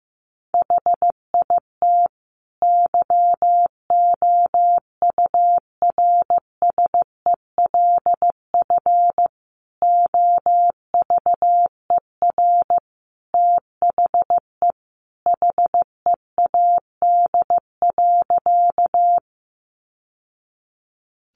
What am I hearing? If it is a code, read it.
Morse code, heard as a keyed tone, about 15 wpm, 700 Hz. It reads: HIT YOURSELF OVER THE HEAD.
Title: Sentences from Top 1200 Words 15wpm